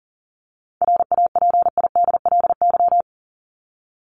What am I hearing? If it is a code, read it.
Morse code, heard as a keyed tone, about 38 wpm, 700 Hz. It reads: RAPIDLY
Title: LY Suffix - Top 100 Words with Base Words 40wpm